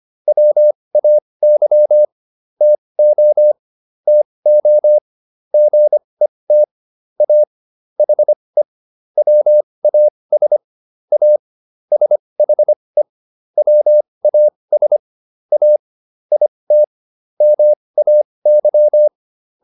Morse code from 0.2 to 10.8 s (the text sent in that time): WAY TO TO GET A HE WAS